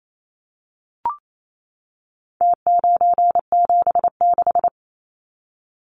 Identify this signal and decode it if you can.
Morse code, heard as a keyed tone, about 28 wpm, 700 Hz. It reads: T976
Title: Numbers - Contest Style - 4-Digit 28wpm